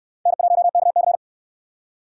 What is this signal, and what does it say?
Morse code, heard as a keyed tone, about 67 wpm, 700 Hz. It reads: N1KC